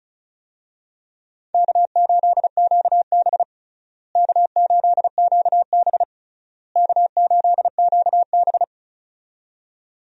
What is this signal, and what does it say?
Morse code, heard as a keyed tone, about 36 wpm, 700 Hz. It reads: K8QB K8QB K8QB
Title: USA Call Signs - 4 Characters - Repeated 3x - 35wpm